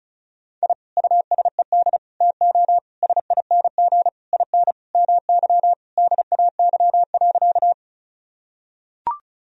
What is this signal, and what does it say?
Morse code, heard as a keyed tone, about 36 wpm, 700 Hz. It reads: I USED TO SING IN MY DAY.
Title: Sentences from Top 1000 Words - Review Part 2 - 35wpm